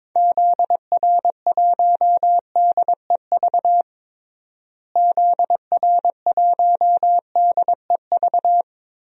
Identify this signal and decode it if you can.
Morse code, heard as a keyed tone, about 22 wpm, 700 Hz. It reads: ZR1DEV ZR1DEV
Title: DXCC-100 Call Signs - 6 Characters - Repeated 3x - 22wpm